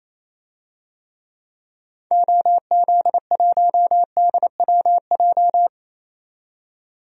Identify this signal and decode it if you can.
Morse code, heard as a keyed tone, about 28 wpm, 700 Hz. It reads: OZ1DWJ
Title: DXCC-100 Call Signs - 6 Characters - 28wpm